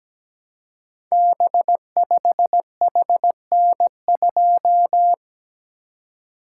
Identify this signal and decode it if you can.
Morse code, heard as a keyed tone, about 17 wpm, 700 Hz. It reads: B5HN2
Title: Code Groups - 5 Letter:Numbers - 17wpm